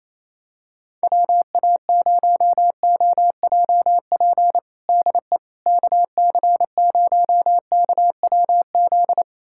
Morse code, heard as a keyed tone, about 28 wpm, 700 Hz. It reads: WA0OJP DE KC0KWZ
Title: USA Callsign DE Callsign - 6 Characters - 28wpm